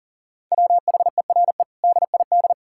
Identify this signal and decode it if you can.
Morse code, heard as a keyed tone, about 40 wpm, 700 Hz. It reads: WHERE DID
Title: Sentences from Top 1600 Words 40wpm